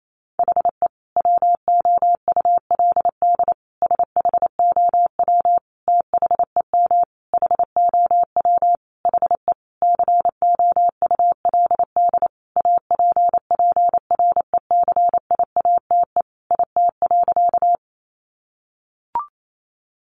Morse code, heard as a keyed tone, about 28 wpm, 700 Hz. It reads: HE WOULD SHOW THEM HOW HE COULD APPRECIATE IT.
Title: Sentences from Top 1300 Words 28wpm